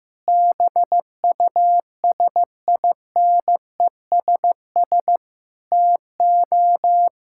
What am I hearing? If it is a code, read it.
Morse code, heard as a keyed tone, about 15 wpm, 700 Hz. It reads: BUSINESS TO
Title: Sentences from Top 1000 Words - Review Part 6 - 15wpm